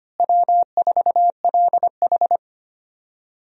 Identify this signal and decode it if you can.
Morse code, heard as a keyed tone, about 25 wpm, 700 Hz. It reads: W4LH
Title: USA Call Signs - 4 Characters - Repeated 2x - 25wpm